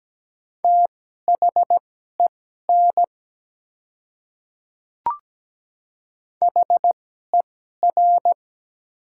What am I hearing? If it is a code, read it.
Morse code, heard as a keyed tone, about 17 wpm, spaced as for 12 wpm, 700 Hz. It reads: THEN HER
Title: 2x Character Spacing - Top 100 Words - 17wpm